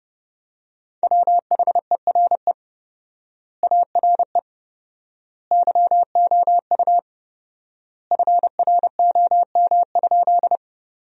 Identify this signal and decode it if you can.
Morse code, heard as a keyed tone, about 32 wpm, 700 Hz. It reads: WHERE ARE YOU FROM?
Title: 4x Word-Spacing - Sentences from Top 200 Words - 30wpm